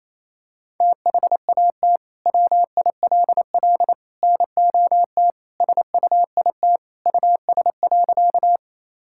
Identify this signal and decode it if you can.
Morse code, heard as a keyed tone, about 28 wpm, 700 Hz. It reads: THAT WILL NOT SUIT US.
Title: Sentences from Top 1600 Words 28wpm